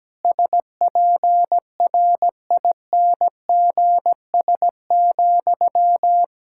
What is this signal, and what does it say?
Morse code, heard as a keyed tone, about 17 wpm, 700 Hz. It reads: SPRINGS,
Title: Practice QSO - Exchange 1 - Beginning - 17wpm